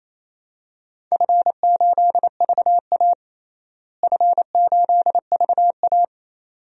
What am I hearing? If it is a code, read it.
Morse code, heard as a keyed tone, about 28 wpm, 700 Hz. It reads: F8VA F8VA
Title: DXCC-100 Call Signs - 4 Characters - Repeated 2x - 28wpm